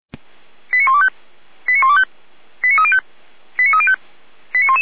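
Five-tone selective call (ZVEI), heard as quick runs of five tones, tones of about 70 ms. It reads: 89125 89125 89305 89305
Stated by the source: ZVEI five-tone call 89277 received by SDR, on a server of the Feuerwehr Orlamünde fire brigade.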